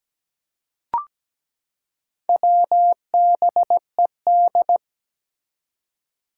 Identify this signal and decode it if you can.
Morse code, heard as a keyed tone, about 17 wpm, 700 Hz. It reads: WBED
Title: Four Challenging Letter-Number Combinations 17wpm